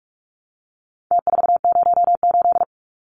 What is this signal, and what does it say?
Morse code, heard as a keyed tone, about 44 wpm, 700 Hz. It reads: T408
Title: Numbers - Contest Style - 4-Digit 45wpm